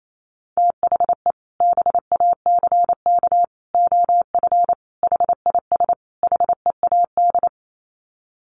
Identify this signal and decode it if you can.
Morse code, heard as a keyed tone, about 28 wpm, 700 Hz. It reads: THE BACK OF HIS HEAD
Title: Sets of 5 Words Encore 28wpm